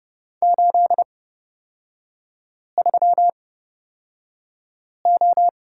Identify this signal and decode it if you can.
Morse code, heard as a keyed tone, about 30 wpm, spaced as for 5 wpm, 700 Hz. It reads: 83O
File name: Lesson 37e - Letter x - Three Characters 30wpm